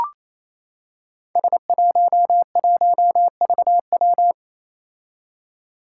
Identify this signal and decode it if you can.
Morse code, heard as a keyed tone, about 27 wpm, 700 Hz. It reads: S11VW
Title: Five Challenging Letter-Number Combinations 28wpm